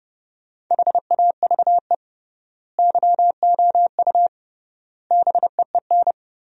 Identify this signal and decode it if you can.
Morse code, heard as a keyed tone, about 30 wpm, 700 Hz. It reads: HAVE YOU BEEN
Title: 3x Word-Spacing - Sentences from Top 200 Words - 30wpm